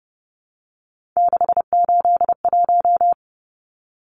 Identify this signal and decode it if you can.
Morse code, heard as a keyed tone, about 30 wpm, 700 Hz. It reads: 681
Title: Numbers - 3-Digit 30wpm